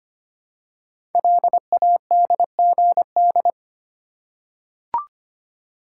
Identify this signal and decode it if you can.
Morse code, heard as a keyed tone, about 25 wpm, 700 Hz. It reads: LADGD
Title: Code Groups - 5 Letters - 25wpm